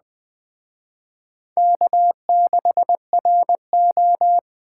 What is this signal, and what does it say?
Morse code, heard as a keyed tone, about 20 wpm, 700 Hz. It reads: K6RO
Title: Speed Racing - USA 1x2 Call Signs 17wpm